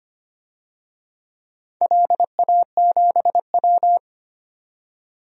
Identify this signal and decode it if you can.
Morse code, heard as a keyed tone, about 25 wpm, 700 Hz. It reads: LA7W
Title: DXCC-100 Call Signs - 4 Characters - 25wpm